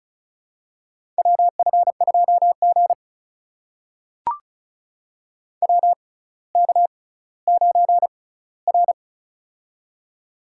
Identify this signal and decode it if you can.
Morse code, heard as a keyed tone, about 36 wpm, spaced as for 18 wpm, 700 Hz. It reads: WF2G WK9R
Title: USA Call Signs - 4 Characters - 6x Character Spacing - 35wpm